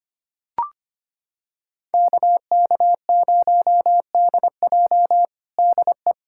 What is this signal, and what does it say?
Morse code, heard as a keyed tone, about 25 wpm, 700 Hz. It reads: KK0DJ DE
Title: USA Callsign DE Callsign - 5 Characters - 25wpm